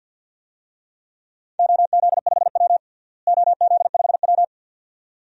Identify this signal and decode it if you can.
Morse code, heard as a keyed tone, about 51 wpm, 700 Hz. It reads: OZ5W OZ5W
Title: DXCC-100 Call Signs - 4 Characters - Repeated 2x - 50wpm